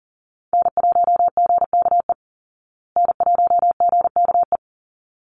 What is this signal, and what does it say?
Morse code, heard as a keyed tone, about 40 wpm, 700 Hz. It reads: N1GKE N1GKE
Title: Call Signs Repeated 3x - 40wpm